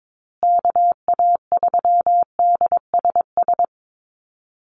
Morse code, heard as a keyed tone, about 22 wpm, 690 Hz. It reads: KA3DSS
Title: USA Call Signs Repeated 3x 22wpm